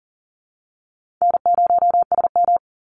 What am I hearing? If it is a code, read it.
Morse code, heard as a keyed tone, about 40 wpm, 700 Hz. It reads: N0SM
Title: Call Signs Repeated 2x - 40wpm